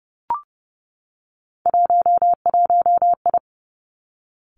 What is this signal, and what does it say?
Morse code, heard as a keyed tone, about 30 wpm, 700 Hz. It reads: 11I
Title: Three Challenging Letter-Number Combinations 30wpm